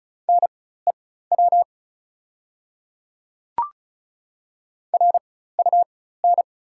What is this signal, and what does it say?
Morse code, heard as a keyed tone, about 36 wpm, spaced as for 16 wpm, 700 Hz. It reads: NEW RUN
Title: 4x Character Spacing - Top 300 Words - 35wpm